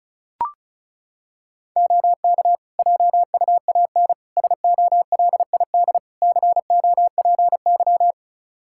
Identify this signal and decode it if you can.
Morse code, heard as a keyed tone, about 36 wpm, 700 Hz. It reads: OK JUAN SOLID COPY
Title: K1USN SST QSO Practice 35wpm